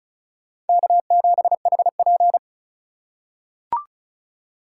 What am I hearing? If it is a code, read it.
Morse code, heard as a keyed tone, about 36 wpm, 700 Hz. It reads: K7HP
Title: USA Call Signs - 4 Characters - 2x Character Spacing - 35wpm